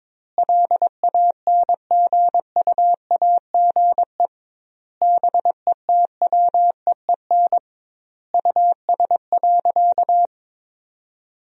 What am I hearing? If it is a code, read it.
Morse code, heard as a keyed tone, about 22 wpm, 700 Hz. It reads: LANGUAGE BETWEEN US.
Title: 2x Word-Spacing - Sentences from Top 500 Words - 22wpm